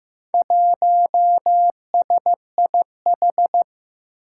Morse code, heard as a keyed tone, about 15 wpm, 700 Hz. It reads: 1SIH
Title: Four Challenging Letter-Number Combinations 15wpm